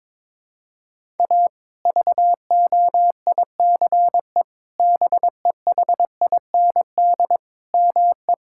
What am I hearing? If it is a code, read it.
Morse code, heard as a keyed tone, about 22 wpm, 700 Hz. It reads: A VOICE BEHIND ME
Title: Sentences from Top 1600 Words 22wpm